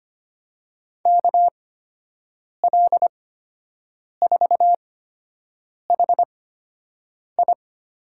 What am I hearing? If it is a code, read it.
Morse code, heard as a keyed tone, about 25 wpm, spaced as for 7 wpm, 700 Hz. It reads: KL4HI
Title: USA Call Signs - 5 Characters - 8x Character Spacing - 25wpm